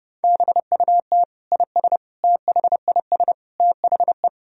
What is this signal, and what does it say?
Morse code, heard as a keyed tone, about 32 wpm, 700 Hz. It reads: BUT IS THIS THE
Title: Sentences from Top 1000 Words - Review Part 3 - 30wpm